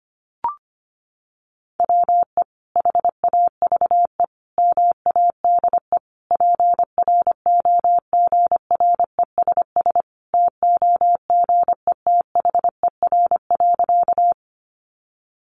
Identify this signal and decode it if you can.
Morse code, heard as a keyed tone, about 25 wpm, 700 Hz. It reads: WE HAVE MADE PROGRESS TOGETHER.
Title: Sentences from Top 1400 Words 25wpm